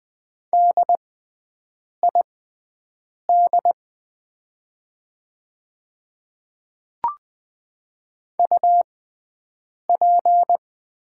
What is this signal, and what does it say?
Morse code, heard as a keyed tone, about 20 wpm, spaced as for 7 wpm, 690 Hz. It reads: DID UP
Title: 6x Character Spacing - Top 100 Words - 20wpm